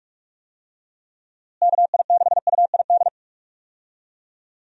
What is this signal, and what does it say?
Morse code, heard as a keyed tone, about 46 wpm, 700 Hz. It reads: KI6UID
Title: USA Call Signs - 6 Characters - Repeated 2x - 45wpm